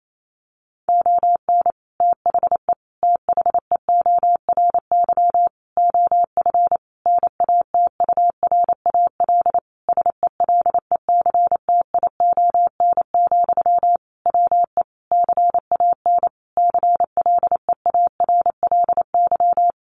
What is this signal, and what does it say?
Morse code, heard as a keyed tone, about 28 wpm, 700 Hz. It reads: ON THE THEORY OF NATURAL SELECTION, WE CAN CLEARLY
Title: Sentences from Top 1400 Words 28wpm